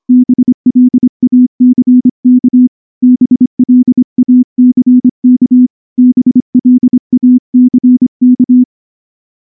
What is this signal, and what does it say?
Morse code, heard as a keyed tone, about 24 wpm, 250 Hz. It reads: BLACK BLACK BLACK
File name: morse / morse-training-25WPM